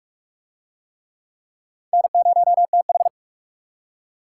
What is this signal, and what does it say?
Morse code, heard as a keyed tone, about 39 wpm, 700 Hz. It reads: N0TH